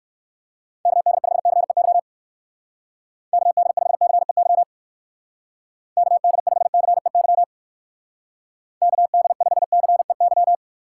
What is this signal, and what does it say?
Morse code, heard as a keyed tone, about 54 wpm, 700 Hz. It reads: KD5CEY KD5CEY KD5CEY KD5CEY
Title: Speed Racing - USA 2x3 Call Signs 45wpm